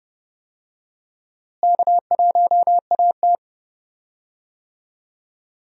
Morse code, heard as a keyed tone, about 32 wpm, 700 Hz. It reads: K1AT
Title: USA Call Signs - 4 Characters - Repeated 3x - 30wpm